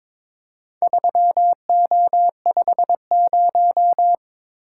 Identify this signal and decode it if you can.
Morse code, heard as a keyed tone, about 22 wpm, 700 Hz. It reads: 3O50
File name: practice / 22wpm / Four Letter-Number Combinations - 22wpm